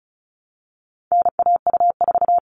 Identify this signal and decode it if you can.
Morse code, heard as a keyed tone, about 34 wpm, 700 Hz. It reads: NAU4
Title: All Cut Numbers - 4-Digit 35wpm